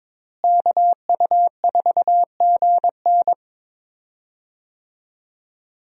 Morse code, heard as a keyed tone, about 22 wpm, 700 Hz. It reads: KU4GN
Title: USA Call Signs - 5 Characters - Repeated 3x - 22wpm